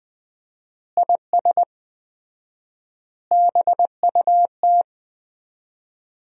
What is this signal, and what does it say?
Morse code, heard as a keyed tone, about 20 wpm, 700 Hz. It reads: IS BUT
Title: 4x Word-Spacing - Sentences from Top 100 Words - 20wpm